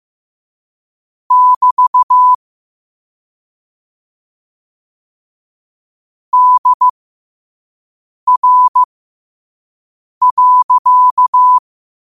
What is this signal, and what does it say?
Morse code, heard as a keyed tone, about 15 wpm, spaced as for 5 wpm, 1000 Hz. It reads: <BT> DR.
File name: qotdfiles / 5wpm